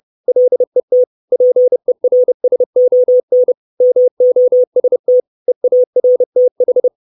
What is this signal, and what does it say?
Morse code, heard as a keyed tone, about 30 wpm, 500 Hz. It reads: LET PERSON MOST EARTH
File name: 632 Words 30WPM